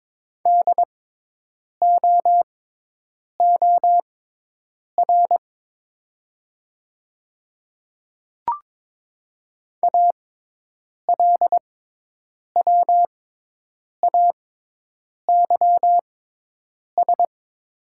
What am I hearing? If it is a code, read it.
Morse code, heard as a keyed tone, about 22 wpm, spaced as for 8 wpm, 700 Hz. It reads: DOOR ALWAYS
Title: 6x Character Spacing - Top 500 Words - 22wpm